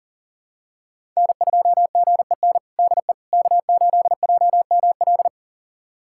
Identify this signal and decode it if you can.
Morse code, heard as a keyed tone, about 40 wpm, 700 Hz. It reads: N2GEN DE K8JML
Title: USA Callsign DE Callsign - 5 Characters - 40wpm